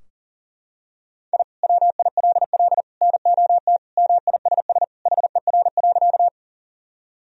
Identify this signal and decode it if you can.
Morse code, heard as a keyed tone, about 40 wpm, 700 Hz. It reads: I WILL NOT MISS HER.
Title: Sentences from Top 1000 Words - Review Part 3 - 40wpm